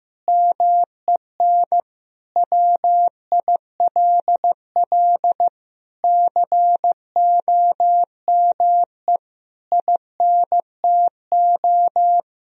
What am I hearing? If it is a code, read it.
Morse code, heard as a keyed tone, about 15 wpm, 700 Hz. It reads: MEN WILL COME INTO
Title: Sentences from Top 1000 Words - Review Part 2 - 15wpm